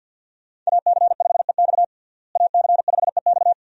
Morse code, heard as a keyed tone, about 52 wpm, 700 Hz. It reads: AC5EX AC5EX
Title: USA Call Signs - 5 Characters - Repeated 2x - 50wpm